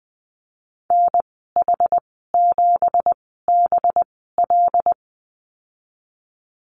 Morse code, heard as a keyed tone, about 20 wpm, spaced as for 14 wpm, 700 Hz. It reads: NH7BL
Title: 213 Lesson 34g - Number 8 - Call Signs 20wpm